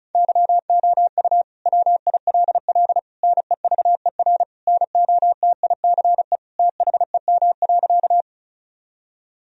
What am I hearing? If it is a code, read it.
Morse code, heard as a keyed tone, about 36 wpm, 700 Hz. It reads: YOU WILL NEVER NOTICE THEM.
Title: Sentences from Top 1000 Words - Review Part 2 - 35wpm